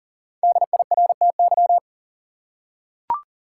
Morse code, heard as a keyed tone, about 40 wpm, 700 Hz. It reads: DIRTY
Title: Five Letter Words - 40wpm